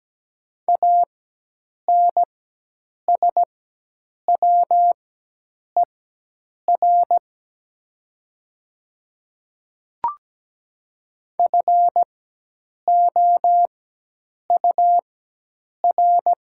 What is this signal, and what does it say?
Morse code, heard as a keyed tone, about 17 wpm, spaced as for 8 wpm, 700 Hz. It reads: ANSWER FOUR
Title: 4x Character Spacing - Top 300 Words - 17wpm